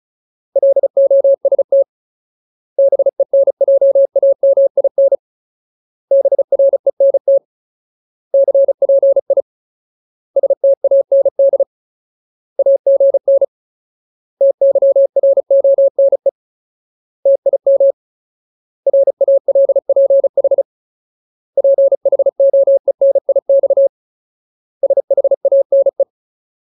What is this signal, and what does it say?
Morse code, heard as a keyed tone, about 34 wpm, 550 Hz. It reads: LOST BENJAMIN BRENT CPI STAND AGN TYRONE TIM RALPH PHOENIX SHANE
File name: Ultimate Word List / 35WPM Ultimate